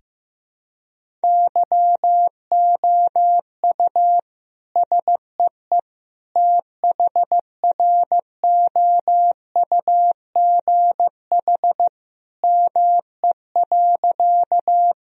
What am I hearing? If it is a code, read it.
Morse code, heard as a keyed tone, about 15 wpm, 700 Hz. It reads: YOU SEE THROUGH ME.
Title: Sentences from Top 1000 Words - Review Part 1 - 15wpm